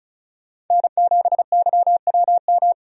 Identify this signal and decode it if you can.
Morse code, heard as a keyed tone, about 34 wpm, 700 Hz. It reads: N7YWM
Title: USA Call Signs - 5 Characters - 35wpm